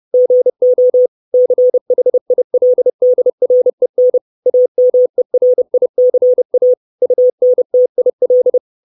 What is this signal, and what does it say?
Morse code, heard as a keyed tone, about 30 wpm, 500 Hz. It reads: GO CHILDREN AMERICA UNTIL
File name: 632 Words 30WPM